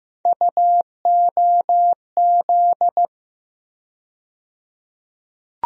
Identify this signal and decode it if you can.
Morse code, heard as a keyed tone, about 15 wpm, 700 Hz. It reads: UOZ
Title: USA Call Sign Postfixes - 3 Letters 15wpm